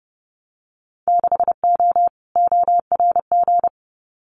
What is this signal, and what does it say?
Morse code, heard as a keyed tone, about 30 wpm, 700 Hz. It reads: 6O ORG
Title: Winter Field Day Exchanges Encore 30wpm